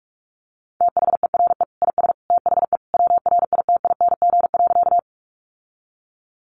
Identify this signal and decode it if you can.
Morse code, heard as a keyed tone, about 44 wpm, 700 Hz. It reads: THERE IS THE WRITING.
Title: Sentences from Top 800 Words 45wpm